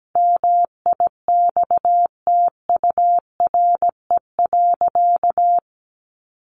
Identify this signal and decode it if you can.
Morse code, heard as a keyed tone, about 17 wpm, 700 Hz. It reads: MIXTURE.
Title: Sentences from Top 1500 Words 17wpm